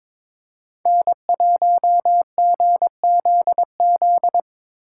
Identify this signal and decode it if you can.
Morse code, heard as a keyed tone, about 22 wpm, 700 Hz. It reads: N1GZZ
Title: USA Call Signs - 5 Characters - 4x Character Spacing - 22wpm